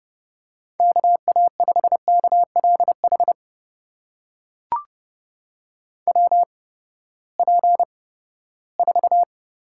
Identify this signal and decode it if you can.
Morse code, heard as a keyed tone, about 30 wpm, spaced as for 17 wpm, 700 Hz. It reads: KA5KLH WP4
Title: USA Call Signs - 6 Characters - 8x Character Spacing - 30wpm